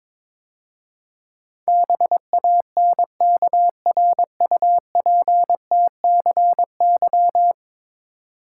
Morse code, heard as a keyed tone, about 22 wpm, 700 Hz. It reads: BANKRUPTCY
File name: practice / 22wpm / 10 Letter Words - First 3 Letters Given - 22wpm